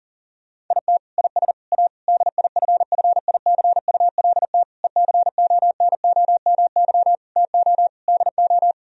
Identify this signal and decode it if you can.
Morse code, heard as a keyed tone, about 40 wpm, 700 Hz. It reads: IT IS A DIFFICULT ECONOMY TO DO
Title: Sentences from Top 1000 Words - Review Part 3 - 40wpm